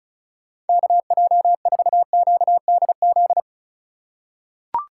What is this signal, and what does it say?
Morse code, heard as a keyed tone, about 36 wpm, 700 Hz. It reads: KJ4QDZ
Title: USA Call Signs - 6 Characters - 4x Character Spacing - 35wpm